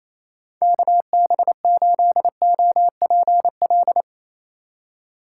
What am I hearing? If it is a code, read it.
Morse code, heard as a keyed tone, about 27 wpm, 700 Hz. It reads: KB8OPL
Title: USA Call Signs - 6 Characters - 2x Character Spacing - 28wpm